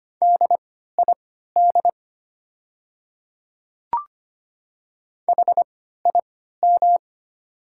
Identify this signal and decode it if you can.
Morse code, heard as a keyed tone, about 25 wpm, spaced as for 14 wpm, 700 Hz. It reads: DID HIM
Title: 3x Character Spacing - Top 100 Words - 25wpm